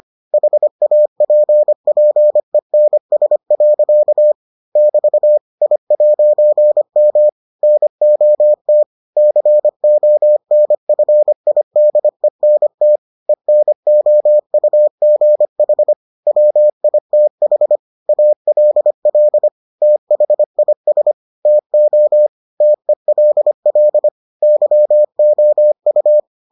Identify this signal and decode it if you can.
Morse code, heard as a keyed tone, about 25 wpm, 600 Hz. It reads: HAPPENS. <BT> I'M NOT CONFIDENT ENOUGH WITH ALL THIS TO TELL YOU